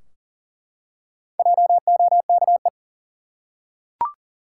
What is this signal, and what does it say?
Morse code, heard as a keyed tone, about 40 wpm, 700 Hz. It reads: JOKE
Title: Four Letter Words - 40wpm